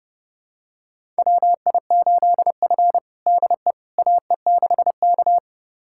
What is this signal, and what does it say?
Morse code, heard as a keyed tone, about 30 wpm, 700 Hz. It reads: WI8F DE AE6K
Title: USA Callsign DE Callsign - 4 Characters - 30wpm